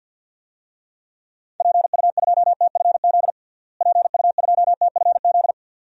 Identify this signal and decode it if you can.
Morse code, heard as a keyed tone, about 52 wpm, 700 Hz. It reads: PU2TFZ PU2TFZ
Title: DXCC-100 Call Signs - 6 Characters - Repeated 2x - 50wpm